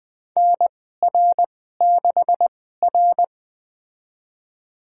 Morse code, heard as a keyed tone, about 20 wpm, spaced as for 15 wpm, 700 Hz. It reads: NR6R
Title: USA Call Signs - 4 Characters - 2x Character Spacing - 20wpm